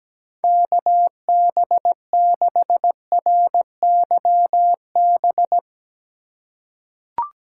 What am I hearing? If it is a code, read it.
Morse code, heard as a keyed tone, about 17 wpm, 700 Hz. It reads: KB6RYB
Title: USA Call Signs - 6 Characters - 4x Character Spacing - 17wpm